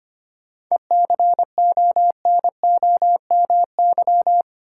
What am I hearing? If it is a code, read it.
Morse code, heard as a keyed tone, about 25 wpm, 700 Hz. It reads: ECONOMY
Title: Sentences from Top 1200 Words - New Word Drill Alt - 25wpm